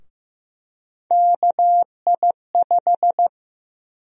Morse code, heard as a keyed tone, about 15 wpm, 700 Hz. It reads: KI5
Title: USA Call Sign Prefixes - 15wpm